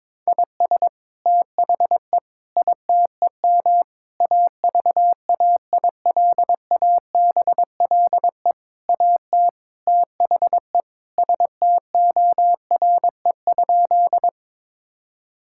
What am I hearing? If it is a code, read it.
Morse code, heard as a keyed tone, about 22 wpm, 700 Hz. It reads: IS THE ITEM AVAILABLE AT THE STORE?
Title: Sentences from Top 1000 Words - Review Part 6 - 22wpm